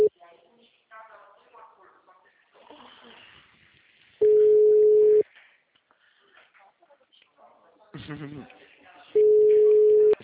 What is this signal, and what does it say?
Ringback tone (Europe, 425 Hz), with speech.